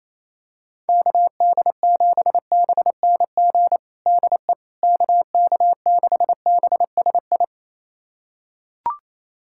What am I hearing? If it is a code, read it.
Morse code, heard as a keyed tone, about 28 wpm, 700 Hz. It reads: KD7BNG DE KK6BSI
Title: USA Callsign DE Callsign - 6 Characters - 28wpm